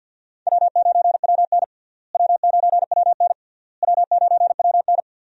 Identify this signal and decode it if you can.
Morse code, heard as a keyed tone, about 49 wpm, 700 Hz. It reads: W9WN W9WN W9WN
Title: USA Call Signs - 4 Characters - Repeated 3x - 50wpm